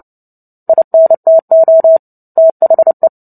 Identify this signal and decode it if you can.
Morse code, heard as a keyed tone, about 30 wpm, 650 Hz. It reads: INTO THE
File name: Book Two / 03-Chapter 3 - 29 WPM